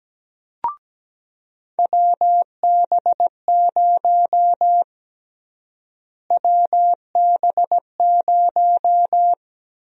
Morse code, heard as a keyed tone, about 17 wpm, 700 Hz. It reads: WB0 WB0
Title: USA Call Sign Prefixes - Repeated 2x - 17wpm